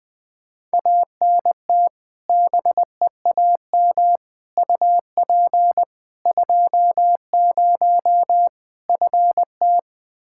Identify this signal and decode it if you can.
Morse code, heard as a keyed tone, about 21 wpm, 700 Hz. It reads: ANT BEAM UP 20 FT
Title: K1USN SST QSO Practice 20wpm